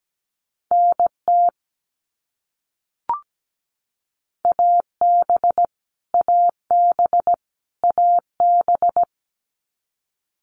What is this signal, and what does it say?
Morse code, heard as a keyed tone, about 17 wpm, 700 Hz. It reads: NT AB AB AB
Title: Canadian Provinces and Territories - Repeated 3x - 17wpm